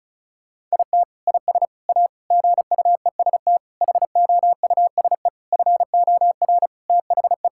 Morse code, heard as a keyed tone, about 34 wpm, 700 Hz. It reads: IT IS A GUEST HOUSE FOR THE
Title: Sentences from Top 1600 Words 35wpm